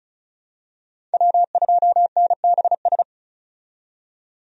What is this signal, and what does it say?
Morse code, heard as a keyed tone, about 34 wpm, 700 Hz. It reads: W2NBS
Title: USA Call Signs - 5 Characters - 35wpm